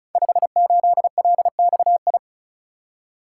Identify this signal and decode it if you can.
Morse code, heard as a keyed tone, about 36 wpm, 700 Hz. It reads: 58LXI
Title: Code Groups - 5 Letter:Numbers - 35wpm